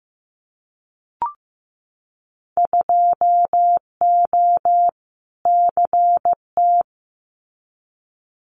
Morse code, heard as a keyed tone, about 15 wpm, 700 Hz. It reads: 2O CT
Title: Winter Field Day Exchanges 15wpm